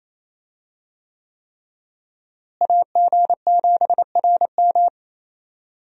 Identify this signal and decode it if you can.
Morse code, heard as a keyed tone, about 28 wpm, 700 Hz. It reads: AG7RM
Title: USA Call Signs - 5 Characters - Repeated 3x - 28wpm